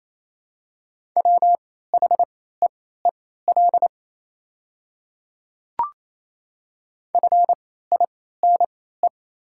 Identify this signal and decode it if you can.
Morse code, heard as a keyed tone, about 28 wpm, spaced as for 16 wpm, 700 Hz. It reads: WHEEL FINE
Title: 3x Character Spacing - Top 500 Words - 28wpm